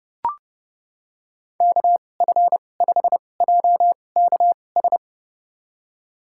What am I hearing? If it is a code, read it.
Morse code, heard as a keyed tone, about 30 wpm, spaced as for 22 wpm, 700 Hz. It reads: KF5JKS
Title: USA Call Signs - 6 Characters - 2x Character Spacing - 30wpm